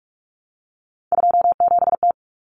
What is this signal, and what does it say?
Morse code, heard as a keyed tone, about 44 wpm, 700 Hz. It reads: 27T
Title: Common Cut Numbers - 3-Digit 45wpm